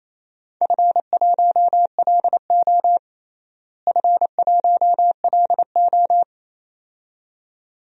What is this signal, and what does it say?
Morse code, heard as a keyed tone, about 28 wpm, 700 Hz. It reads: F1LO F1LO
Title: DXCC-100 Call Signs - 4 Characters - Repeated 2x - 28wpm